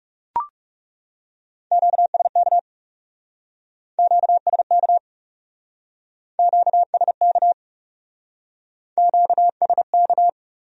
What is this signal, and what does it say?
Morse code, heard as a keyed tone, about 37 wpm, 700 Hz. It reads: QSK QSK QSK QSK
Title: Speed-Racing Q-Codes - 30wpm